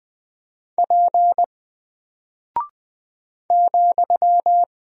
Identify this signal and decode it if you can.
Morse code, heard as a keyed tone, about 20 wpm, 700 Hz. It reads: P ,
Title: Single Character (Letter:Number:Punctuation:Prosign) 20wpm